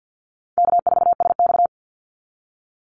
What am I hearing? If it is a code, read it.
Morse code, heard as a keyed tone, about 50 wpm, 700 Hz. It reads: K4SX
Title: Call Signs - 50wpm